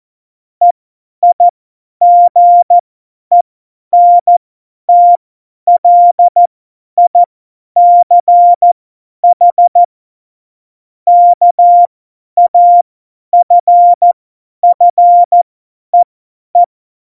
Morse code, heard as a keyed tone, about 14 wpm, spaced as for 10 wpm, 700 Hz. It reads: EIGENTLICH KAFFEE
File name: Morsekurs / Witze / Witz 18. Juni 2019, 50 F70